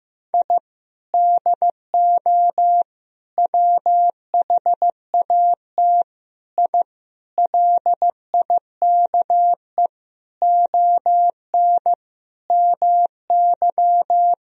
Morse code, heard as a keyed tone, about 16 wpm, 700 Hz. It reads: I DO WHAT I LIKE ON MY
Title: Sentences from Top 1000 Words - Review Part 2 - 15wpm